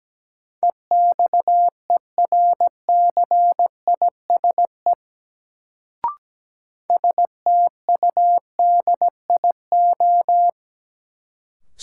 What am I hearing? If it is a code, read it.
Morse code, heard as a keyed tone, about 17 wpm, 700 Hz. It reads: EXERCISE STUDIO
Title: Sentences from Top 1200 Words - New Word Drill Alt - 17wpm